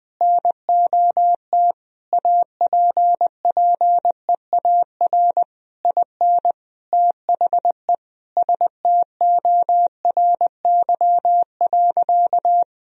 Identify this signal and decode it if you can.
Morse code, heard as a keyed tone, about 21 wpm, 700 Hz. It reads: NOT APPEAR IN THE STORY.
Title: Sentences from Top 1000 Words - Review Part 2 - 20wpm